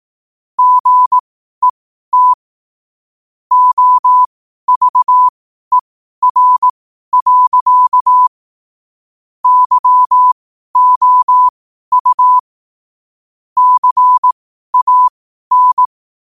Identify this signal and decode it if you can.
Morse code, heard as a keyed tone, about 18 wpm, spaced as for 12 wpm, 1000 Hz. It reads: GET OVER. YOU CAN
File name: qotdfiles / 12wpm